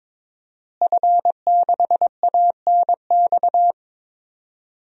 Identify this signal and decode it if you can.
Morse code, heard as a keyed tone, about 22 wpm, 700 Hz. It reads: F6ANX
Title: DXCC-100 Call Signs - 5 Characters - Repeated 2x - 22wpm